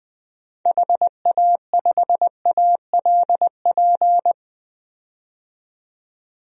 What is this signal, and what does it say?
Morse code, heard as a keyed tone, about 20 wpm, 700 Hz. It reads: HA5ALP
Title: DXCC-100 Call Signs - 6 Characters - 20wpm